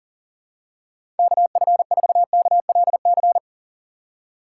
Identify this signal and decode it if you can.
Morse code, heard as a keyed tone, about 37 wpm, 700 Hz. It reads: KF4KLC
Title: USA Call Signs - 6 Characters - 2x Character Spacing - 40wpm